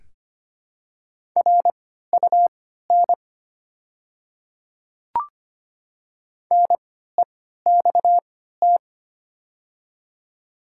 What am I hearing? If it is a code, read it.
Morse code, heard as a keyed tone, about 25 wpm, spaced as for 14 wpm, 700 Hz. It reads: RUN NEXT
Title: 3x Character Spacing - Top 300 Words - 25wpm